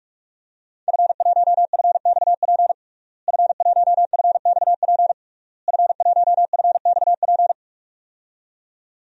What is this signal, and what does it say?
Morse code, heard as a keyed tone, about 46 wpm, 700 Hz. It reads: F1FXP F1FXP F1FXP
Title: DXCC-100 Call Signs - 5 Characters - Repeated 3x - 45wpm